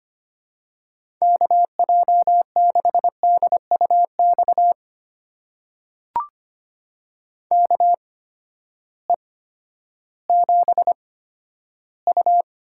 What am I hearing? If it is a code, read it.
Morse code, heard as a keyed tone, about 25 wpm, spaced as for 12 wpm, 700 Hz. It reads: KJ6DUX KE7U
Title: USA Call Signs - 6 Characters - 8x Character Spacing - 25wpm